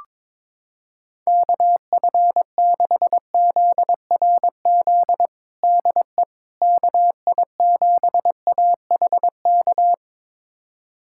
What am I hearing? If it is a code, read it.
Morse code, heard as a keyed tone, about 22 wpm, 700 Hz. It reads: KF6ZRZ DE KI7AHK
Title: USA Callsign DE Callsign 22wpm